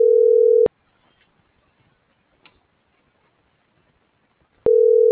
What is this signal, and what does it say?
Ringback tone (North America), 440 + 480 Hz.